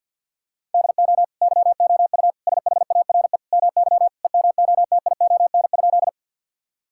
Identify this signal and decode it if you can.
Morse code, heard as a keyed tone, about 50 wpm, 700 Hz. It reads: DO YOU SHARE MY EMOTION?
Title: Sentences from Top 1600 Words 50wpm